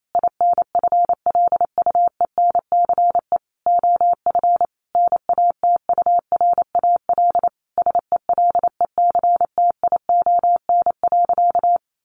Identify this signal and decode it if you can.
Morse code, heard as a keyed tone, about 28 wpm, 700 Hz. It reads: INFLUENCE OF NATURAL SELECTION.
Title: Sentences from Top 1400 Words 28wpm